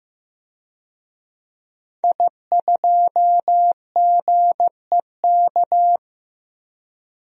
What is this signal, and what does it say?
Morse code, heard as a keyed tone, about 15 wpm, 700 Hz. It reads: I2GEK
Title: DXCC-100 Call Signs - 5 Characters - 15wpm